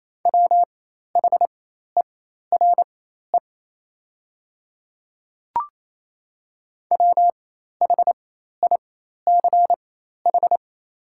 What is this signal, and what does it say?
Morse code, heard as a keyed tone, about 28 wpm, spaced as for 13 wpm, 700 Hz. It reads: WHERE WHICH
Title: 4x Character Spacing - Top 200 Words - 28wpm